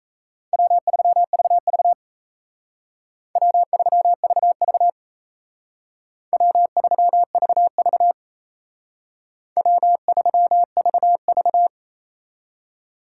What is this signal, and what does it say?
Morse code, heard as a keyed tone, about 33 wpm, 700 Hz. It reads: W3VV W3VV W3VV W3VV
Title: Speed Racing - USA 1x2 Call Signs 28wpm